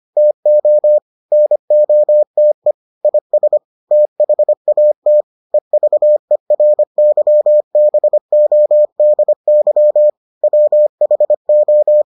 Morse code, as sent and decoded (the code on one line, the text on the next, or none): - --- / -. --- - . / .. ... / - .... .- - / . ...- . .-. -.-- -... --- -.. -.-- / .-- .... ---
TO NOTE IS THAT EVERYBODY WHO